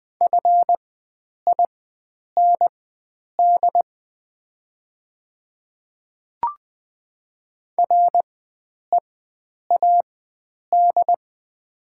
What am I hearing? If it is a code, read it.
Morse code, heard as a keyed tone, about 20 wpm, spaced as for 9 wpm, 700 Hz. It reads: FIND READ
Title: 4x Character Spacing - Top 300 Words - 20wpm